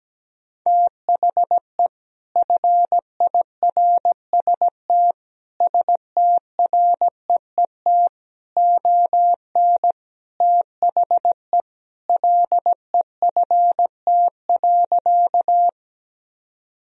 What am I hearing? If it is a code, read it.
Morse code, heard as a keyed tone, about 17 wpm, 700 Hz. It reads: THE FIRST STREET ON THE LEFT.
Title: Sentences from Top 1000 Words - Review Part 3 - 17wpm